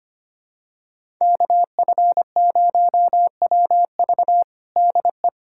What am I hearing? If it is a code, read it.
Morse code, heard as a keyed tone, about 25 wpm, 700 Hz. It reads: KF0WV DE
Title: USA Callsign DE Callsign 25wpm